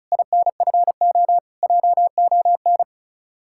Morse code, heard as a keyed tone, about 46 wpm, 700 Hz. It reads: INFO JON